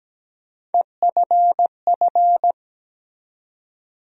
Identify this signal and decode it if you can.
Morse code, heard as a keyed tone, about 17 wpm, 700 Hz. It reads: EFF
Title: USA Call Sign Postfixes - 3 Letters 17wpm